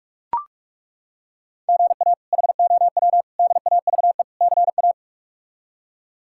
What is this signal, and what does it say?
Morse code, heard as a keyed tone, about 41 wpm, 700 Hz. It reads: GA HOW DAVE CA
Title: K1USN SST QSO Practice - 45wpm